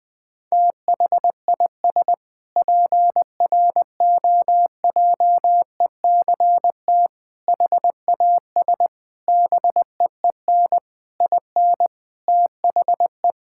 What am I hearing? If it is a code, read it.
Morse code, heard as a keyed tone, about 20 wpm, 700 Hz. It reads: THIS PROJECT HAS BEEN IN THE
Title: Sentences with 6-Word Phrases - Spoken Beginning - 20wpm